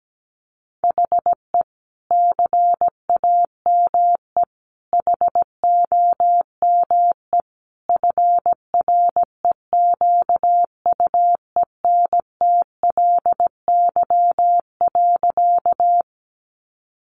Morse code, heard as a keyed tone, about 17 wpm, 700 Hz. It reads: HE CAME HOME FREQUENTLY.
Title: Sentences from Top 1300 Words 17wpm